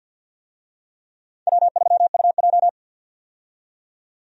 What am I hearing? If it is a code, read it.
Morse code, heard as a keyed tone, about 49 wpm, 700 Hz. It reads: W3UJ
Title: USA Call Signs - 4 Characters - 50wpm